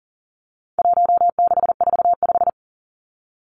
Morse code, heard as a keyed tone, about 40 wpm, 700 Hz. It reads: 1645
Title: Numbers - 4-Digit 40wpm